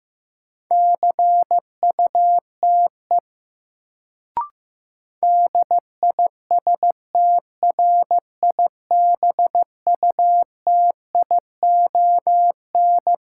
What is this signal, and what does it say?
Morse code, heard as a keyed tone, about 15 wpm, 700 Hz. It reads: CUTE DISTRIBUTION
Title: Sentences from Top 1000 Words - Review Part 7 - New Word Drill - 15wpm